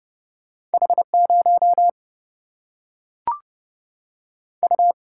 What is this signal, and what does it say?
Morse code, heard as a keyed tone, about 30 wpm, 700 Hz. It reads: H0 U
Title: Single-Letter Single-Number 30wpm